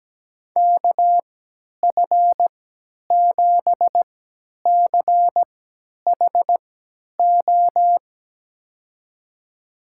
Morse code, heard as a keyed tone, about 17 wpm, spaced as for 10 wpm, 700 Hz. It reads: KF7CHO
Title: USA Call Signs - 6 Characters - 3x Character Spacing - 17wpm